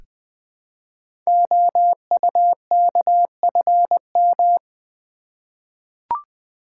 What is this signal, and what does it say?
Morse code, heard as a keyed tone, about 20 wpm, 700 Hz. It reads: OUKFM
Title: Code Groups - 5 Letters - 20wpm